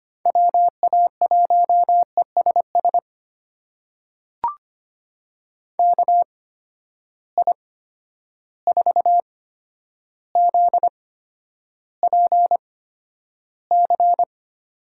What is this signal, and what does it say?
Morse code, heard as a keyed tone, about 25 wpm, spaced as for 11 wpm, 700 Hz. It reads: WA1ESS KI4ZPC
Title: USA Call Signs - 6 Characters - 8x Character Spacing - 25wpm